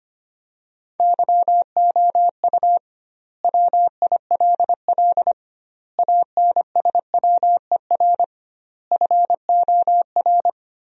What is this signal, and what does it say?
Morse code, heard as a keyed tone, about 25 wpm, 700 Hz. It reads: YOU WILL ANSWER FOR
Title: 2x Word-Spacing - Sentences from Top 300 Words - 25wpm